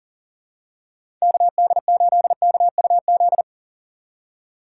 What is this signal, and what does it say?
Morse code, heard as a keyed tone, about 40 wpm, 700 Hz. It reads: KD8KUZ
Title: USA Call Signs - 6 Characters - 4x Character Spacing - 40wpm